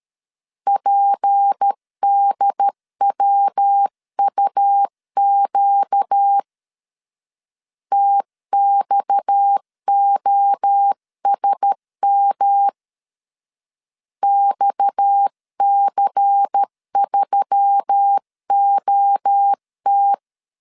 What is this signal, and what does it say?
Morse code, heard as a keyed tone, about 13 wpm, 800 Hz. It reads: PDWUQ TXOSM XC3OT
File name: CW / DET / 1 6 1500C S C 12wpm 3533